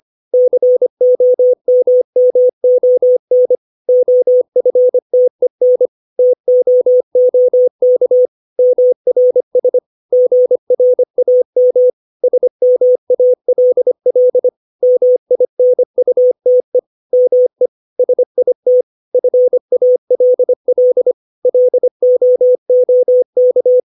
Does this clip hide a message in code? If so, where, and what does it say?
Yes: Morse code from 0.3 to 23.8 s, heard as a keyed tone, about 25 wpm, 500 Hz. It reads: COMMON OFTEN TOOK MRS GRAM SMALL MINUTE ME SIT FALL LOOK